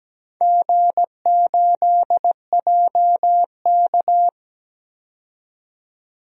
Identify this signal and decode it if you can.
Morse code, heard as a keyed tone, about 17 wpm, 700 Hz. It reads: G8JK
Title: DXCC-100 Call Signs - 4 Characters - 17wpm